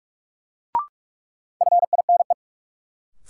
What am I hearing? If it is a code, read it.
Morse code, heard as a keyed tone, about 46 wpm, 700 Hz. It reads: FINE
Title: Sentences from Top 500 Words - New Word Drill - 45wpm